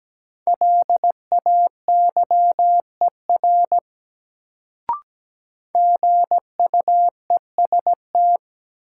Morse code, heard as a keyed tone, about 17 wpm, 700 Hz. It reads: LAYER GUEST
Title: Sentences from Top 1600 Words - New Word Drill - 17wpm